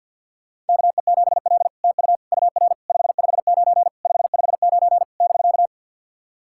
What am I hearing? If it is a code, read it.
Morse code, heard as a keyed tone, about 48 wpm, 700 Hz. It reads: KE7L TU UR 559 559 <BK>
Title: SOTA QSO Practice 50wpm